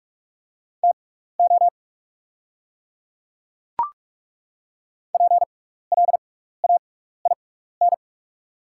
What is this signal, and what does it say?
Morse code, heard as a keyed tone, about 42 wpm, spaced as for 15 wpm, 690 Hz. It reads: TO PLAIN
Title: 6x Character Spacing - Top 500 Words - 45wpm